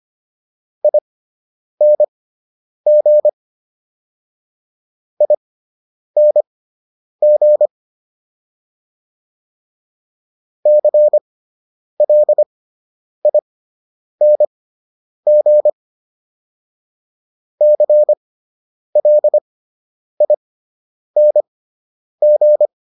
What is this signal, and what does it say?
Morse code, heard as a keyed tone, about 25 wpm, spaced as for 9 wpm, 600 Hz. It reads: ING ING CLING CLING
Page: /fundamental-practice-files/